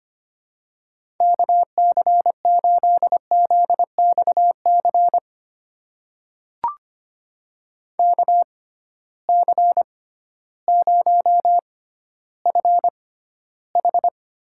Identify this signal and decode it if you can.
Morse code, heard as a keyed tone, about 25 wpm, spaced as for 13 wpm, 700 Hz. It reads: KC8ZXC KC0FH
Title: USA Call Signs - 6 Characters - 6x Character Spacing - 25wpm